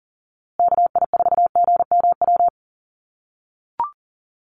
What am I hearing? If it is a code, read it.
Morse code, heard as a keyed tone, about 38 wpm, 700 Hz. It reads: KI4GMW
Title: USA Call Signs Repeated 3x 40wpm